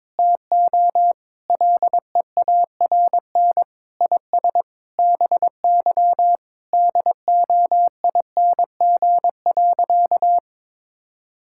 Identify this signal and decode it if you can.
Morse code, heard as a keyed tone, about 22 wpm, 700 Hz. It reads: TO LEARN IS BY DOING.
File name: practice / 22wpm / Sentences with 4-Word Phrases - 22wpm